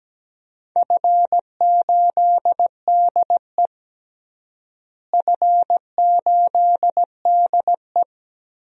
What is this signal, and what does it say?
Morse code, heard as a keyed tone, about 17 wpm, 700 Hz. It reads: F8DE F8DE
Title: DXCC-100 Call Signs - 4 Characters - Repeated 3x - 17wpm